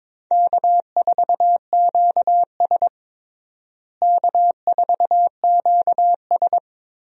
Morse code, heard as a keyed tone, about 22 wpm, 700 Hz. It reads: K4QS K4QS
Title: USA Call Signs - 4 Characters - Repeated 3x - 22wpm